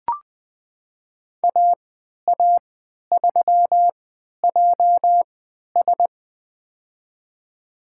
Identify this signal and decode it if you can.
Morse code, heard as a keyed tone, about 20 wpm, spaced as for 11 wpm, 700 Hz. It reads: AA3JS
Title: USA Call Signs - 5 Characters - 3x Character Spacing - 20wpm